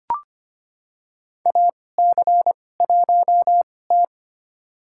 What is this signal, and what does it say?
Morse code, heard as a keyed tone, about 25 wpm, spaced as for 18 wpm, 700 Hz. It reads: AC1T
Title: USA Call Signs - 2x Character Spacing - 25wpm